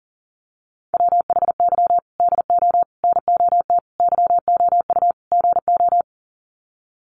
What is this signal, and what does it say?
Morse code, heard as a keyed tone, about 40 wpm, 700 Hz. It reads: WHY DO NOT YOU GO
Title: Sets of 5 Words Encore 40wpm